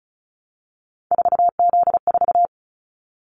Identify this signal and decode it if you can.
Morse code, heard as a keyed tone, about 36 wpm, 700 Hz. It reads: 4Z4
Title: Callsign Prefixes for Top 50 DXCC - 35wpm